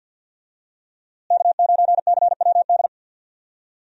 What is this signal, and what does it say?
Morse code, heard as a keyed tone, about 51 wpm, 700 Hz. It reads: K9CWD
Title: USA Call Signs - 5 Characters - Repeated 2x - 50wpm